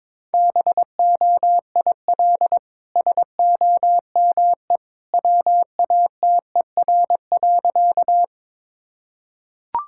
Morse code, heard as a keyed tone, about 23 wpm, 700 Hz. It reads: BOIL SOME WATER.
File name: practice / 22wpm / Easy Sentences with Hints - 22wpm